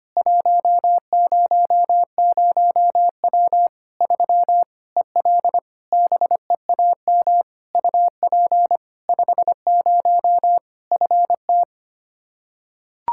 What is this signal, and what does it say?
Morse code, heard as a keyed tone, about 25 wpm, 700 Hz. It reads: 100W 3 EL BEAM UP 50 FT
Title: Practice QSO - Exchange 2 - Info - 25wpm